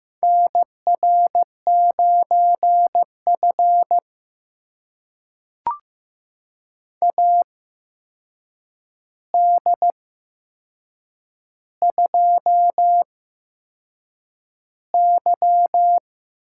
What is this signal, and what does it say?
Morse code, heard as a keyed tone, about 15 wpm, spaced as for 6 wpm, 700 Hz. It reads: NR9F AD2Y
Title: USA Call Signs - 4 Characters - 8x Character Spacing - 15wpm